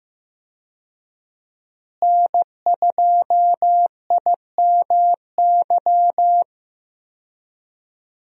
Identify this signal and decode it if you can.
Morse code, heard as a keyed tone, about 15 wpm, 700 Hz. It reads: N2IMY
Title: USA Call Signs - 5 Characters - Repeated 2x - 15wpm